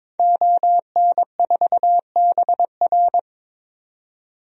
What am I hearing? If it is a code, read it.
Morse code, heard as a keyed tone, about 21 wpm, 700 Hz. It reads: ON4BR
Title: DXCC-100 Call Signs - 5 Characters - 22wpm